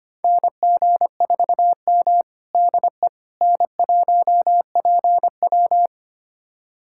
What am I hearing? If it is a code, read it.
Morse code, heard as a keyed tone, about 25 wpm, 700 Hz. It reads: NG4M DE N1PW
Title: USA Callsign DE Callsign - 4 Characters - 25wpm